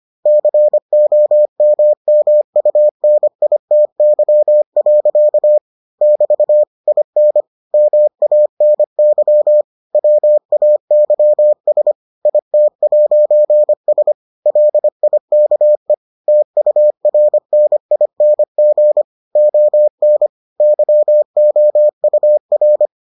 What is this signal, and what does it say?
Morse code, heard as a keyed tone, about 25 wpm, 600 Hz. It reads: COMMUNITY. <BT> IN MANY WAYS IT'S LIKE TURNING ON YOUR